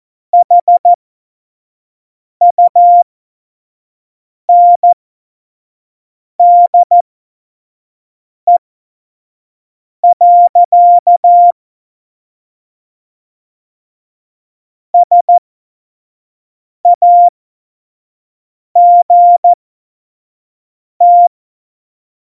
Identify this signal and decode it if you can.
Morse code, heard as a keyed tone, about 14 wpm, spaced as for 5 wpm, 700 Hz. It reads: HUNDE. SAGT
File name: Morsekurs / Witze / Witz 31.Juli 2018, 25 F70